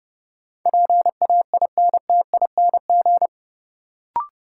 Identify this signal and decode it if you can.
Morse code, heard as a keyed tone, about 30 wpm, 700 Hz. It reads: PAINTING
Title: Sentences from Top 1000 Words - Review Part 6 - New Word Drill - 30wpm